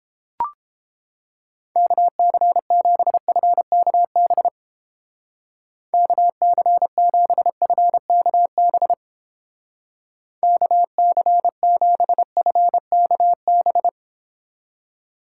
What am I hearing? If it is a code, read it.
Morse code, heard as a keyed tone, about 29 wpm, 700 Hz. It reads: KC7FKB KC7FKB KC7FKB